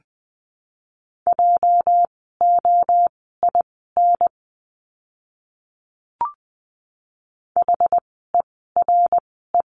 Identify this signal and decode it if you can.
Morse code, heard as a keyed tone, about 20 wpm, spaced as for 14 wpm, 700 Hz. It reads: JOIN HERE
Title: 206 Lesson 33f - Letter j - Words 20wpm